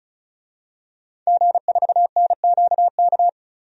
Morse code, heard as a keyed tone, about 36 wpm, 700 Hz. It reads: G4NQK
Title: DXCC-100 Call Signs - 5 Characters - 35wpm